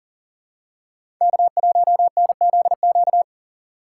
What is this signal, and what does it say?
Morse code, heard as a keyed tone, about 40 wpm, 700 Hz. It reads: K1NZQ
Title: USA Call Signs - 3x Character Spacing - 40wpm